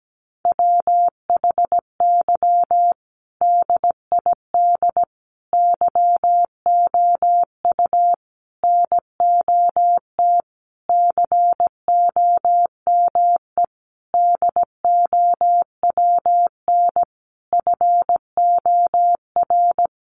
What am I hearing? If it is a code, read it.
Morse code, heard as a keyed tone, about 17 wpm, 700 Hz. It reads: WHY DID YOU NOT COME DOWN FOR
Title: Sentences from Top 1500 Words 17wpm